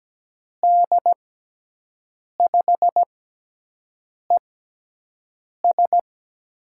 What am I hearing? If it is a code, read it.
Morse code, heard as a keyed tone, about 17 wpm, spaced as for 6 wpm, 700 Hz. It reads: D5ES
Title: USA Call Signs - 6x Character Spacing - 17wpm